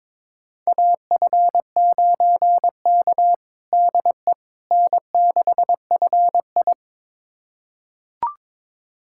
Morse code, heard as a keyed tone, about 22 wpm, 700 Hz. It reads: AF9K DE N6FI
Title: USA Callsign DE Callsign - 4 Characters - 22wpm